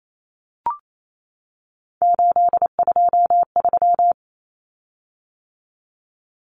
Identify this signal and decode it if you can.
Morse code, heard as a keyed tone, about 28 wpm, 700 Hz. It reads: 823
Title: Common Cut Numbers - 3-Digit 28wpm